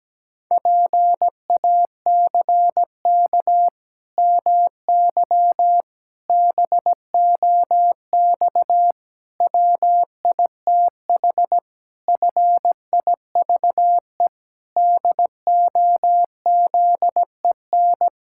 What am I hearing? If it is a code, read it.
Morse code, heard as a keyed tone, about 17 wpm, 700 Hz. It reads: PACK MY BOX WITH FIVE DOZEN
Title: Pangram Challenge v2 - 17wpm